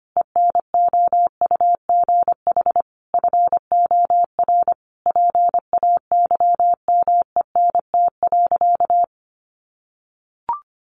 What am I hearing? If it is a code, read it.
Morse code, heard as a keyed tone, about 25 wpm, 700 Hz. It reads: ENOUGH FOR PAYMENT.
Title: Sentences from Top 1200 Words 25wpm